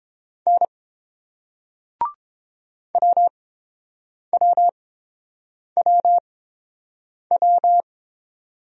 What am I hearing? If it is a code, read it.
Morse code, heard as a keyed tone, about 30 wpm, spaced as for 7 wpm, 700 Hz. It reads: N WWWW